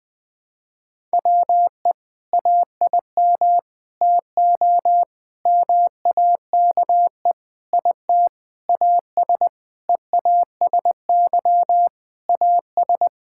Morse code, heard as a keyed tone, about 21 wpm, 700 Hz. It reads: WE AIM TO MAKE IT AS EASY AS
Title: Sentences with 7-Word Phrases - 20wpm